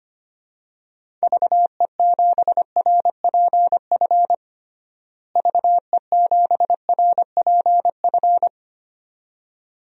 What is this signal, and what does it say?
Morse code, heard as a keyed tone, about 25 wpm, 700 Hz. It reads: VE7RPF VE7RPF
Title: DXCC-100 Call Signs - 6 Characters - Repeated 2x - 25wpm